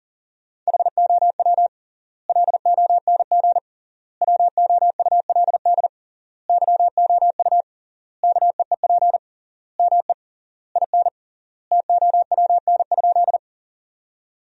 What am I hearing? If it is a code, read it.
Morse code, heard as a keyed tone, about 40 wpm, 700 Hz. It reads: HOW LONG WOULD YOU KEEP ME IN TOWN?
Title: 3x Word-Spacing - Sentences from Top 500 Words - 40wpm